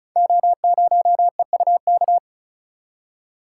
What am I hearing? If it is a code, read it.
Morse code, heard as a keyed tone, about 36 wpm, 700 Hz. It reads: O0EUK
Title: Code Groups - 5 Letter:Numbers - 35wpm